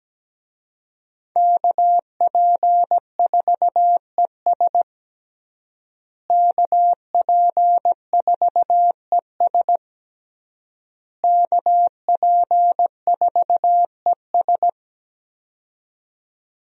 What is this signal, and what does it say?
Morse code, heard as a keyed tone, about 17 wpm, 700 Hz. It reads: KP4ES KP4ES KP4ES
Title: USA Call Signs - 5 Characters - Repeated 3x - 17wpm